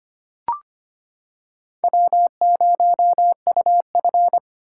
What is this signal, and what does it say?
Morse code, heard as a keyed tone, about 25 wpm, 700 Hz. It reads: W0UF
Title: K1USN SST QSO Practice - 25wpm